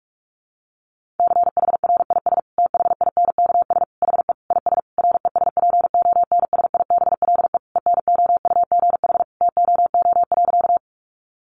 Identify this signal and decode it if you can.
Morse code, heard as a keyed tone, about 44 wpm, 700 Hz. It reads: CHRIS THINKS HE IS RESPONSIBLE ENOUGH TOO.
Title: Easy Phrases 45wpm